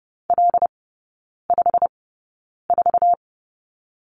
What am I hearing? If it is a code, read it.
Morse code, heard as a keyed tone, about 30 wpm, spaced as for 9 wpm, 700 Hz. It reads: L54
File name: Lesson 15e - Number 5 - Three Characters 30wpm